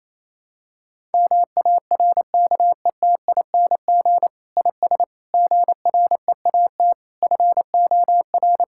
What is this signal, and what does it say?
Morse code, heard as a keyed tone, about 28 wpm, 700 Hz. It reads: MARKETING IS GREAT FOR
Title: Sentences from Top 1000 Words - Review Part 5 - 28wpm